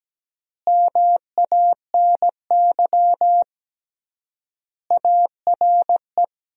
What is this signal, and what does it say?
Morse code, heard as a keyed tone, about 17 wpm, 700 Hz. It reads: MANY ARE
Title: 3x Word-Spacing - Sentences from Top 100 Words - 17wpm